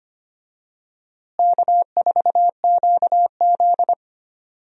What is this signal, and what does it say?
Morse code, heard as a keyed tone, about 25 wpm, 700 Hz. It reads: K4QZ
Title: USA Call Signs - 4 Characters - Repeated 3x - 25wpm